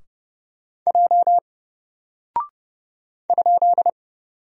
Morse code, heard as a keyed tone, about 29 wpm, 700 Hz. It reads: J ?
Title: Single Character (Letter:Number:Punctuation:Prosign) 30wpm